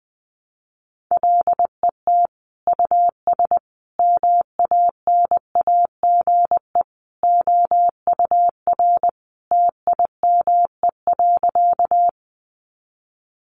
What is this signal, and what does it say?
Morse code, heard as a keyed tone, about 20 wpm, 700 Hz. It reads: LET US MANAGE OUR TIME.
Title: Sentences from Top 1400 Words 20wpm